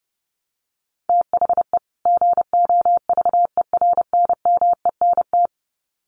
Morse code, heard as a keyed tone, about 30 wpm, 700 Hz. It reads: THE GOVERNMENT
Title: Sets of 2 Words 30wpm